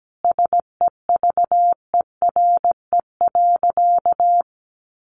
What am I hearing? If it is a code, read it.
Morse code, heard as a keyed tone, about 17 wpm, 700 Hz. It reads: SEVERE.
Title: Sentences from Top 1400 Words 17wpm